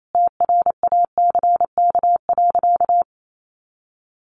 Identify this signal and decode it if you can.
Morse code, heard as a keyed tone, about 28 wpm, 700 Hz. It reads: TRACK.
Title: Sentences from Top 1100 Words 28wpm